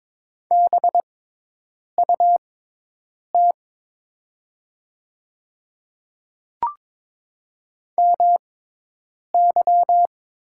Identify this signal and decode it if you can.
Morse code, heard as a keyed tone, about 22 wpm, spaced as for 8 wpm, 700 Hz. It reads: BUT MY
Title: 6x Character Spacing - Top 200 Words - 22wpm